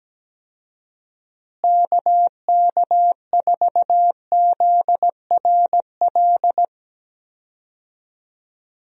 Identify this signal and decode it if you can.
Morse code, heard as a keyed tone, about 17 wpm, 700 Hz. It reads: KK4ZRL
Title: USA Call Signs - 6 Characters - Repeated 2x - 17wpm